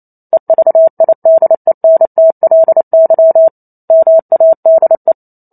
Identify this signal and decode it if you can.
Morse code, heard as a keyed tone, about 28 wpm, 650 Hz. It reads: EVIDENTLY MADE